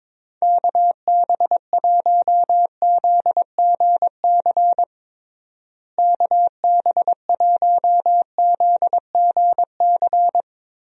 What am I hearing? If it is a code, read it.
Morse code, heard as a keyed tone, about 21 wpm, 700 Hz. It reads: KB1ZGC KB1ZGC
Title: USA Call Signs - 6 Characters - Repeated 2x - 22wpm